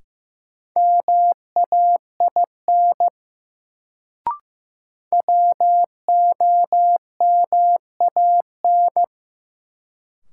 Morse code, heard as a keyed tone, about 15 wpm, 700 Hz. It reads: MAIN WOMAN
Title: Sentences from Top 1000 Words - Review Part 2 - New Word Drill - 15wpm